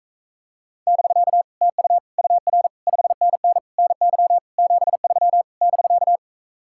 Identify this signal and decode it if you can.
Morse code, heard as a keyed tone, about 45 wpm, 700 Hz. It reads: <BK> TU UR 5NN NY 73 <BK>